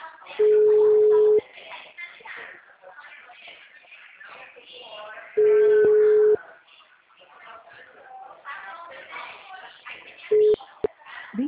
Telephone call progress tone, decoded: ringback tone

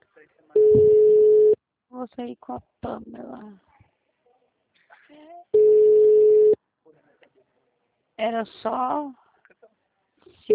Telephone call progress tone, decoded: ringback tone

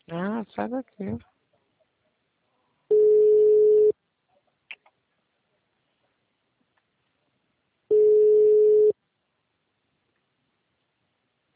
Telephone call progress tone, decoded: ringback tone

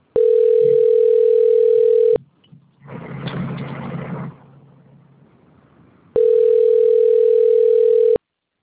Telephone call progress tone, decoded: ringback tone

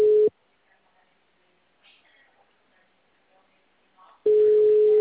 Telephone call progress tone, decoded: ringback tone

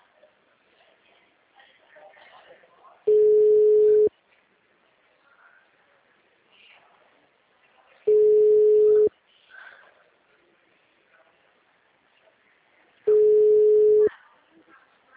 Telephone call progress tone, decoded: ringback tone

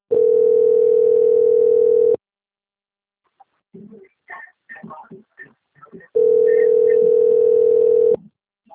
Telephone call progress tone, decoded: ringback tone

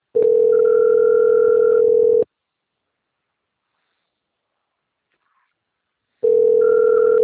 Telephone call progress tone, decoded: ringback tone

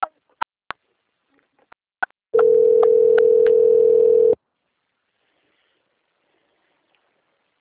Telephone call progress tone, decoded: ringback tone

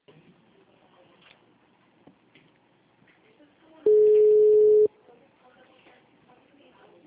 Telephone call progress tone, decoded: ringback tone

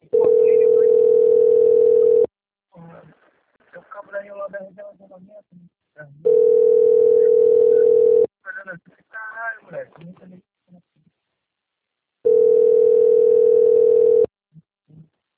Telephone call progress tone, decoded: ringback tone

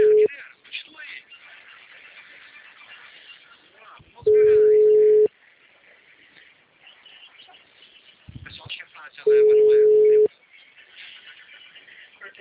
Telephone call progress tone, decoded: ringback tone

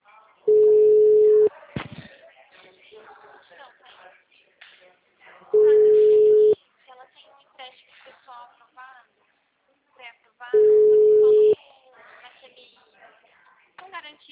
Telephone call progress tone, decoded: ringback tone

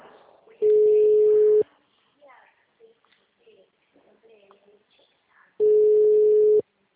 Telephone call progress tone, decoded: ringback tone

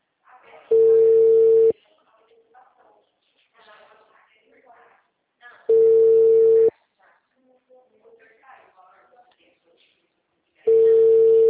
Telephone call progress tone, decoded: ringback tone

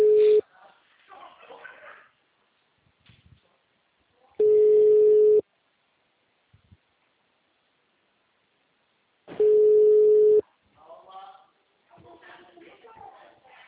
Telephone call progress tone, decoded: ringback tone

